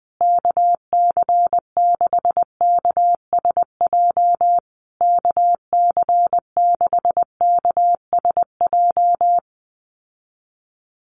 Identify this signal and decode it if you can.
Morse code, heard as a keyed tone, about 19 wpm, 700 Hz. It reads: KC6KSJ KC6KSJ
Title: USA Call Signs Repeated 3x 20wpm